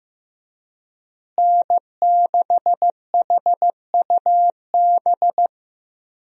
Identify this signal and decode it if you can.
Morse code, heard as a keyed tone, about 15 wpm, 700 Hz. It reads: N6HUB
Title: USA Call Signs - 5 Characters - Repeated 3x - 15wpm